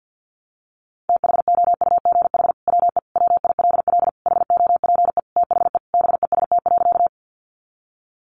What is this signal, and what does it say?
Morse code, heard as a keyed tone, about 50 wpm, 700 Hz. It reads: THOUGH WE WILL HOPE THE BEST.
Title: Sentences from Top 1000 Words 50wpm